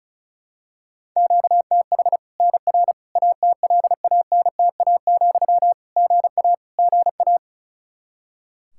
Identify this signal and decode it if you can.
Morse code, heard as a keyed tone, about 36 wpm, 700 Hz. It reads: QTH NR ATLANTA, GA GA
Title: Practice QSO - Exchange 1 - Beginning - 35wpm